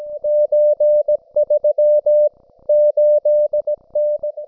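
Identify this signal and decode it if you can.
Morse code (Russian alphabet), heard as a keyed tone, about 17 wpm, 600 Hz. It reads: Ч3 8Н
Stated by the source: Russian military Morse station, M32, RDL, on 5268 kHz.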